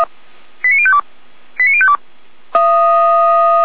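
Five-tone selective call (ZVEI), heard as quick runs of five tones, tones of about 70 ms. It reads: 89052 89052
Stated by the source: ZVEI five-tone selective call 89058 received by SDR, from the Feuerwehr Orlamünde audio server.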